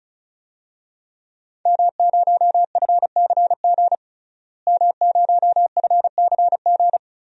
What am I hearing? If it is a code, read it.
Morse code, heard as a keyed tone, about 36 wpm, 700 Hz. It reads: M0FCG M0FCG
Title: DXCC-100 Call Signs - 5 Characters - Repeated 2x - 35wpm